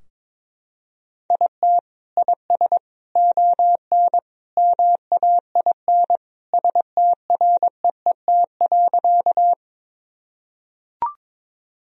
Morse code, heard as a keyed tone, about 22 wpm, 700 Hz. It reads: IT IS ON MAIN STREET.
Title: Sentences from Top 1000 Words - Review Part 2 - 22wpm